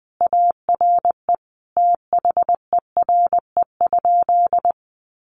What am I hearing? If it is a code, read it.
Morse code, heard as a keyed tone, about 20 wpm, 700 Hz. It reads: ARE THERE?
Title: Sentences from Top 100 Words 20wpm